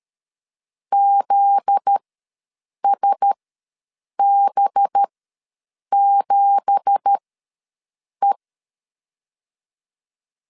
Morse code, heard as a keyed tone, about 13 wpm, spaced as for 7 wpm, 800 Hz. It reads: ZSB7E
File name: LEZIONI / 9 2867 ws 12wpm 3431